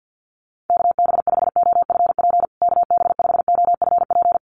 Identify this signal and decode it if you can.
Morse code, heard as a keyed tone, about 50 wpm, 700 Hz. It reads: KB5OFP KB5OFP
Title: USA Call Signs Repeated 2x 50wpm